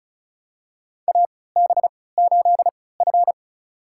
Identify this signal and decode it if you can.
Morse code, heard as a keyed tone, about 36 wpm, spaced as for 20 wpm, 700 Hz. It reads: AB8F
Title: USA Call Signs - 4 Characters - 3x Character Spacing - 35wpm